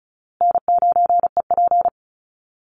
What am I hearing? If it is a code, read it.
Morse code, heard as a keyed tone, about 34 wpm, 700 Hz. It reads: N9EP
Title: Easy Call Signs Repeated 3x - 35wpm